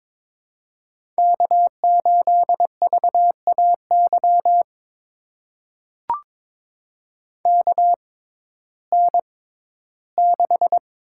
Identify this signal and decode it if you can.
Morse code, heard as a keyed tone, about 22 wpm, spaced as for 13 wpm, 700 Hz. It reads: K8VAY KN6
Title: USA Call Signs - 6x Character Spacing - 22wpm